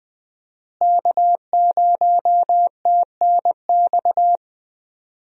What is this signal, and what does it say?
Morse code, heard as a keyed tone, about 21 wpm, 700 Hz. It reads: K0TNX
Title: USA Call Signs - 5 Characters - Repeated 3x - 20wpm